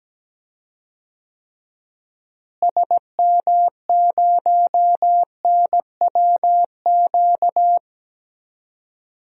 Morse code, heard as a keyed tone, about 17 wpm, 700 Hz. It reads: SM0NWQ
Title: DXCC-100 Call Signs - 6 Characters - Repeated 3x - 17wpm